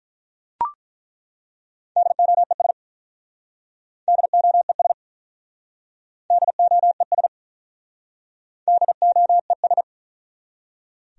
Morse code, heard as a keyed tone, about 48 wpm, 700 Hz. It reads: DOES DOES DOES DOES